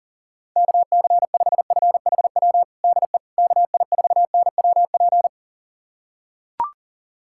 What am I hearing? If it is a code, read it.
Morse code, heard as a keyed tone, about 39 wpm, 700 Hz. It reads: KC5FHW DE KI4NWP
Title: USA Callsign DE Callsign 40wpm